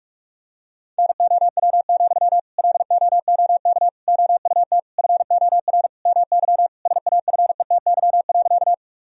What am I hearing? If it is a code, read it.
Morse code, heard as a keyed tone, about 42 wpm, 700 Hz. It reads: NOW, LOOK OUT FOR MY SAFETY.
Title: Sentences from Top 1000 Words - Review Part 4 - 45wpm